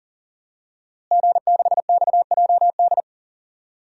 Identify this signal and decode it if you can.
Morse code, heard as a keyed tone, about 40 wpm, 700 Hz. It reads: G6XJD
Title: DXCC-100 Call Signs - 5 Characters - Repeated 2x - 40wpm